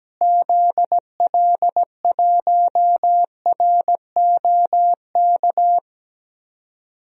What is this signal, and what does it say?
Morse code, heard as a keyed tone, about 17 wpm, 700 Hz. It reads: ZL1ROK
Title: DXCC-100 Call Signs - 6 Characters - Repeated 3x - 17wpm